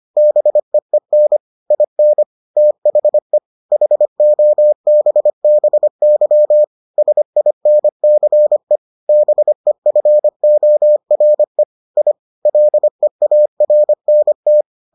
Morse code, heard as a keyed tone, about 25 wpm, 600 Hz. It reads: BEEN IN THE HOBBY SINCE BEFORE I LEARNT